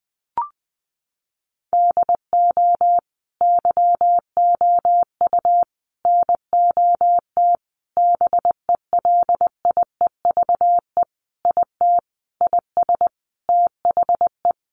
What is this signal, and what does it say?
Morse code, heard as a keyed tone, about 20 wpm, 700 Hz. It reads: DO YOU NOT BELIEVE IT IS THE
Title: Sentences from Top 1000 Words 20wpm